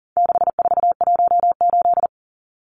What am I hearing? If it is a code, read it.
Morse code, heard as a keyed tone, about 40 wpm, 700 Hz. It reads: 6418
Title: Numbers - 4-Digit 40wpm